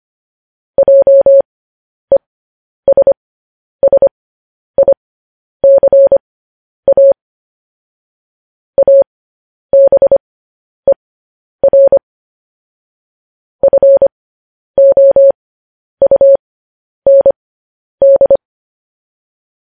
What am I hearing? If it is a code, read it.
Morse code, heard as a keyed tone, about 25 wpm, spaced as for 10 wpm, 550 Hz. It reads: JESSICA ABER FOUND
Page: news-headlines-in-morse-code-at-10-wpm